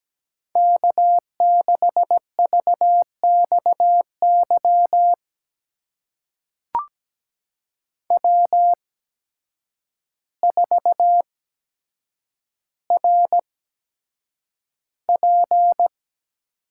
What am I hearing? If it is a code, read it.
Morse code, heard as a keyed tone, about 17 wpm, spaced as for 8 wpm, 700 Hz. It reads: K6VXY W4RP
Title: USA Call Signs - 5 Characters - 8x Character Spacing - 17wpm